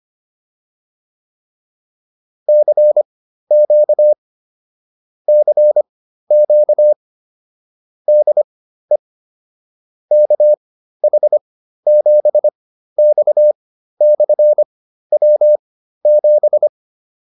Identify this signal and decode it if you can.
Morse code, heard as a keyed tone, about 25 wpm, spaced as for 13 wpm, 600 Hz. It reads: CQ CQ DE KH7X/W7
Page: intermediate-practice-files